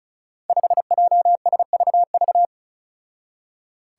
Morse code, heard as a keyed tone, about 36 wpm, 700 Hz. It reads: 5JSVV
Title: Five Challenging Letter-Number Combinations 35wpm